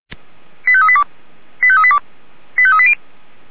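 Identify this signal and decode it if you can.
Five-tone selective call (ZVEI), heard as quick runs of five tones, tones of about 70 ms. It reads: 85382 85382 85380
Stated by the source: ZVEI five-tone selective call received by SDR, from the Feuerwehr Orlamünde fire brigade audio archive.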